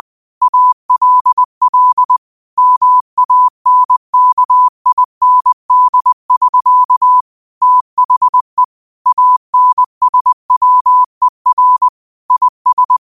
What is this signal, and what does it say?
Morse code, heard as a keyed tone, about 20 wpm, 1000 Hz. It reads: ALL MANKIND<SK> THE ANSWER IS